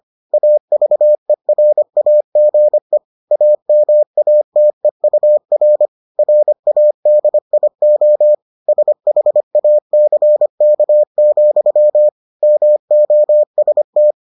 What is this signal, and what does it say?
Morse code, heard as a keyed tone, about 25 wpm, 600 Hz. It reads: AVERAGE AMATEUR RADIO SHACK, MOST